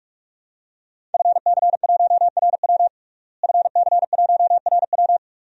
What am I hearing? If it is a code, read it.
Morse code, heard as a keyed tone, about 44 wpm, 700 Hz. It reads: FC1RW FC1RW
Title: DXCC-100 Call Signs - 5 Characters - Repeated 3x - 45wpm